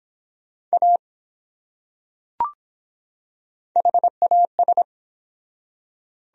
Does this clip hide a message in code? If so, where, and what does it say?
Yes: Morse code from 0.7 to 4.8 s, heard as a keyed tone, about 24 wpm, 700 Hz. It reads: A HAS